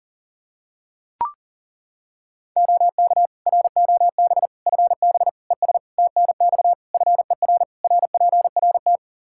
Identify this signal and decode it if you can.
Morse code, heard as a keyed tone, about 40 wpm, 700 Hz. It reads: OK ROB FB ES TNX FER RPRT
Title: Practice QSO - Exchange 2 - Info - 40wpm